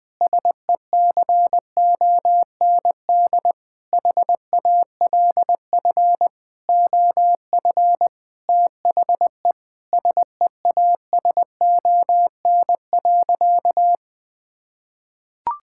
Morse code, heard as a keyed tone, about 20 wpm, 700 Hz. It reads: SECOND HALF OF THE SEASON.
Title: Sentences with 7-Word Phrases - Spoken Beginning - 20wpm